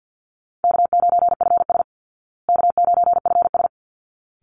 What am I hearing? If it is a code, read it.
Morse code, heard as a keyed tone, about 48 wpm, 700 Hz. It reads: K9FS K9FS
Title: Easy Call Signs Repeated 2x - 50wpm